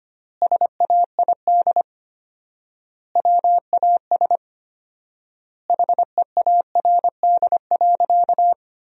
Morse code, heard as a keyed tone, about 25 wpm, 700 Hz. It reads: SAID WAS HEARD.
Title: 4x Word-Spacing - Sentences from Top 400 Words - 25wpm